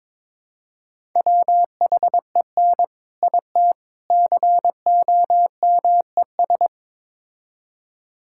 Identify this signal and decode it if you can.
Morse code, heard as a keyed tone, about 22 wpm, 700 Hz. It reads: WHEN IT COMES
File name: practice / 22wpm / Build A Phrase - 5-Word Phrases - 22wpm